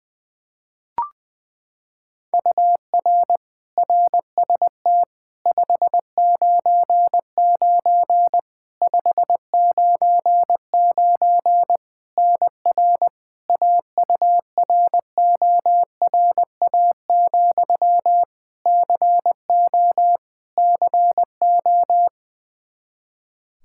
Morse code, heard as a keyed tone, about 20 wpm, 700 Hz. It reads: UR RST 599 599 NR AURORA, CO CO
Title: K1USN SST QSO Practice 20wpm